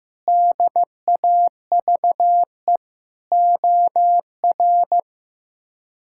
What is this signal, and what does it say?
Morse code, heard as a keyed tone, about 15 wpm, 700 Hz. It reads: DAVE OR
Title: K1USN SST QSO Practice - 15wpm